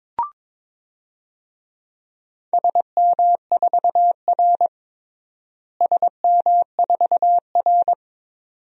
Morse code, heard as a keyed tone, about 22 wpm, 700 Hz. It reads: SM4R SM4R
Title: DXCC-100 Call Signs - 4 Characters - Repeated 2x - 22wpm